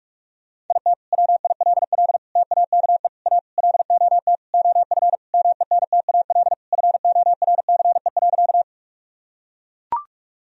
Morse code, heard as a keyed tone, about 42 wpm, 700 Hz. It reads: IT WILL TAKE A LOT OF MENTAL FORCE.
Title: Sentences from Top 1000 Words - Review Part 4 - 45wpm